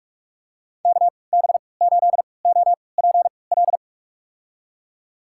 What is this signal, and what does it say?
Morse code, heard as a keyed tone, about 46 wpm, spaced as for 26 wpm, 700 Hz. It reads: KB8OPL
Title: USA Call Signs - 6 Characters - 3x Character Spacing - 45wpm